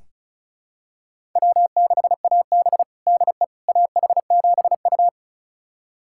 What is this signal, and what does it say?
Morse code, heard as a keyed tone, about 36 wpm, 700 Hz. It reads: W6AB DE AH7U
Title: USA Callsign DE Callsign - 4 Characters - 35wpm